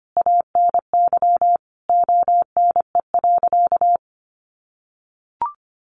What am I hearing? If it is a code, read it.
Morse code, heard as a keyed tone, about 25 wpm, 700 Hz. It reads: ANY ONE.
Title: Sentences from Top 1000 Words 25wpm